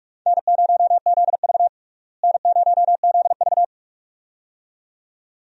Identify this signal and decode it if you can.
Morse code, heard as a keyed tone, about 46 wpm, 700 Hz. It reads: N0ZV N0ZV
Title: USA Call Signs - 4 Characters - Repeated 2x - 45wpm